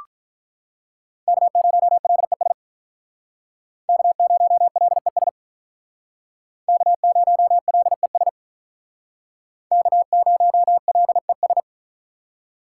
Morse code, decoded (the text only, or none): K0LES K0LES K0LES K0LES